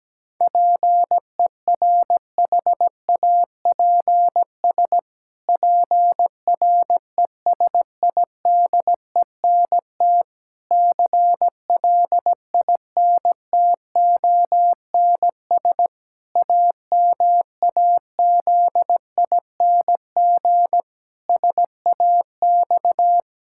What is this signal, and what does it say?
Morse code, heard as a keyed tone, about 17 wpm, 700 Hz. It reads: PERHAPS PRESIDENT CLINTONS AMAZING SAX
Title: Pangram Challenge v2 - 17wpm